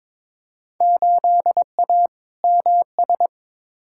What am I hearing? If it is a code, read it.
Morse code, heard as a keyed tone, about 22 wpm, 700 Hz. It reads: 8A MS
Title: Field Day Exchanges - 22wpm